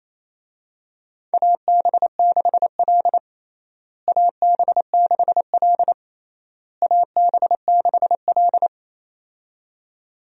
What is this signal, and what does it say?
Morse code, heard as a keyed tone, about 28 wpm, 700 Hz. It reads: AB6L AB6L AB6L
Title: USA Call Signs - 4 Characters - Repeated 3x - 28wpm